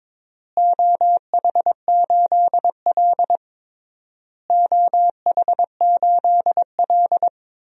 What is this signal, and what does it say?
Morse code, heard as a keyed tone, about 22 wpm, 700 Hz. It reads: OH8L OH8L
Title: DXCC-100 Call Signs - 4 Characters - Repeated 3x - 22wpm